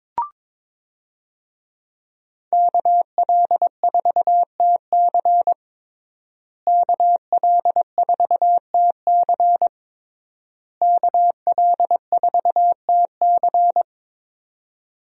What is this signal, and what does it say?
Morse code, heard as a keyed tone, about 23 wpm, 700 Hz. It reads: KL4TC KL4TC KL4TC
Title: USA Call Signs - 5 Characters - Repeated 3x - 22wpm